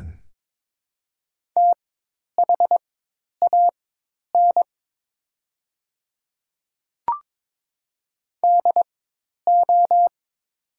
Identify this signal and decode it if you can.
Morse code, heard as a keyed tone, about 23 wpm, spaced as for 10 wpm, 700 Hz. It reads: THAN DO
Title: 4x Character Spacing - Top 100 Words - 22wpm